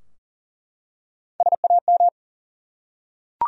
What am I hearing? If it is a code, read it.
Morse code, heard as a keyed tone, about 40 wpm, 700 Hz. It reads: SAM
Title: USA Common Names - Max 3 Letters - 40wpm